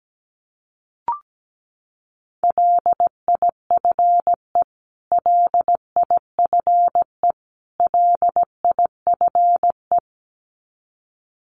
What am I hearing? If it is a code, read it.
Morse code, heard as a keyed tone, about 17 wpm, 700 Hz. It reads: LIFE LIFE LIFE
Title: Top 500 Words Repeated 17wpm